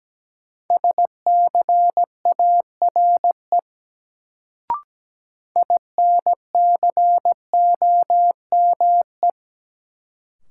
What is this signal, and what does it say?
Morse code, heard as a keyed tone, about 17 wpm, 700 Hz. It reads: SCARE INCOME
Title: Sentences from Top 1000 Words - Review Part 5 - New Word Drill - 17wpm